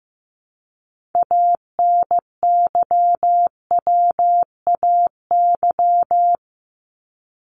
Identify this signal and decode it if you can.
Morse code, heard as a keyed tone, about 15 wpm, 700 Hz. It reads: ANYWAY
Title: Top 1501-1750 Words 15wpm